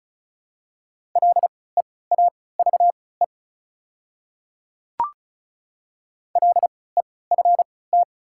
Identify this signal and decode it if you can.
Morse code, heard as a keyed tone, about 36 wpm, spaced as for 20 wpm, 690 Hz. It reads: LEAVE LEFT
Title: 3x Character Spacing - Top 500 Words - 35wpm